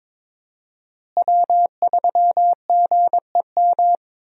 Morse code, heard as a keyed tone, about 22 wpm, 700 Hz. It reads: W3GEM
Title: USA Call Signs - 5 Characters - Repeated 2x - 22wpm